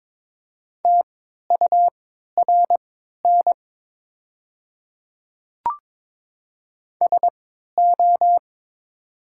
Morse code, heard as a keyed tone, about 22 wpm, spaced as for 13 wpm, 700 Hz. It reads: TURN SO
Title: 3x Character Spacing - Top 200 Words - 22wpm